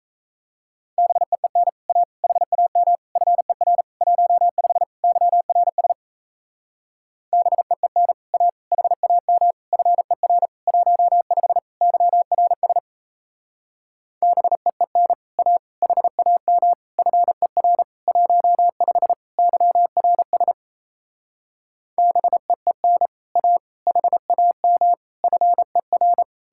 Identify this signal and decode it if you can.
Morse code, heard as a keyed tone, about 36 wpm, 700 Hz. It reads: BEEN A HAM FER 15 YRS BEEN A HAM FER 15 YRS BEEN A HAM FER 15 YRS BEEN A HAM FER